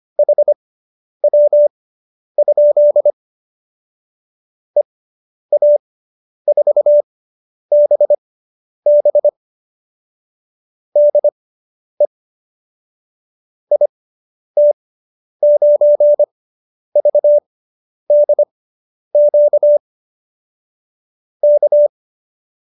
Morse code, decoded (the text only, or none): HW? EA4BB DE IT9VDQ K